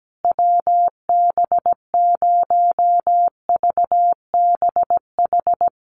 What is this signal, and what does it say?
Morse code, heard as a keyed tone, about 17 wpm, 700 Hz. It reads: WB0VBH
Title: USA Call Signs Repeated 3x 17wpm